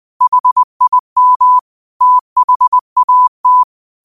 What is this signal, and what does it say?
Morse code, heard as a keyed tone, about 20 wpm, 1000 Hz. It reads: HIM THAT
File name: 20wpm